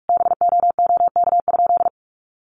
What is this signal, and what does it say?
Morse code, heard as a keyed tone, about 44 wpm, 700 Hz. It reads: BOOK?
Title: Sentences from Top 1300 Words 45wpm